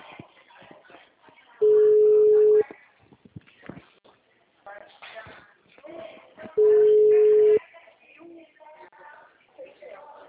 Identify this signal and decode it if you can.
Ringback tone (Europe, 425 Hz), with speech.